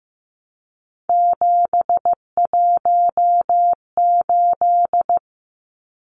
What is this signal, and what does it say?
Morse code, heard as a keyed tone, about 15 wpm, 700 Hz. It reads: 718
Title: Numbers - 3-Digit 15wpm